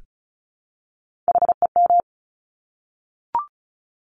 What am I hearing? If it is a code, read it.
Morse code, heard as a keyed tone, about 36 wpm, 700 Hz. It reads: HEM
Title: Top Three Letter Combinations in Words 35wpm